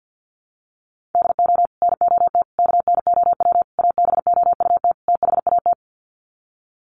Morse code, heard as a keyed tone, about 48 wpm, 700 Hz. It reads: DO NOT KNOW ABOUT THAT
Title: Sets of 5 Words Encore2 50wpm